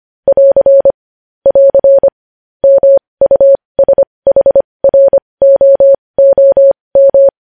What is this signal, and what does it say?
Morse code, heard as a keyed tone, about 25 wpm, spaced as for 20 wpm, 550 Hz. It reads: <AR> <AR> MUSHROOM
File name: podcast / News headlines-1x-25x20